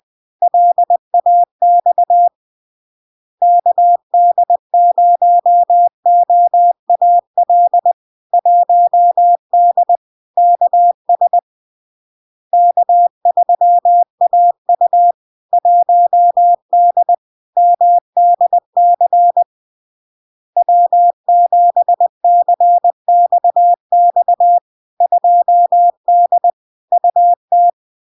Morse code, heard as a keyed tone, about 20 wpm, 700 Hz. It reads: LAX KD0OAL 1D KS K3AU 1D MDC W7CXX 2D UT
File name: LARK-FD-2022-Log-in-CW-20wpm-700-Hz